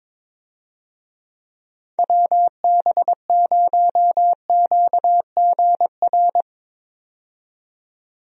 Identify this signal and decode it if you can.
Morse code, heard as a keyed tone, about 22 wpm, 700 Hz. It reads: WB0QGR
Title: USA Call Signs - 6 Characters - 22wpm